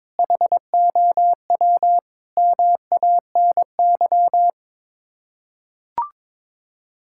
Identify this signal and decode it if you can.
Morse code, heard as a keyed tone, about 22 wpm, 700 Hz. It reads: HOW MANY
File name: practice / 22wpm / Sets of 6 Words - 22wpm